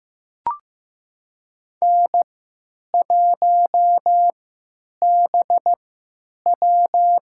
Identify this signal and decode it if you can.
Morse code, heard as a keyed tone, about 15 wpm, spaced as for 9 wpm, 700 Hz. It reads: N1BW
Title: USA Call Signs - 4 Characters - 3x Character Spacing - 15wpm